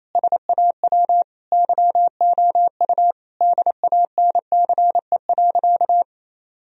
Morse code, heard as a keyed tone, about 28 wpm, 700 Hz. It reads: SAW YOU DANCE.
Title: Sentences from Top 1000 Words - Review Part 4 - 28wpm